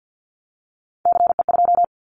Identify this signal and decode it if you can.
Morse code, heard as a keyed tone, about 50 wpm, 700 Hz. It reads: CE2
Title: Callsign Prefixes for Top 100 DXCC - 50wpm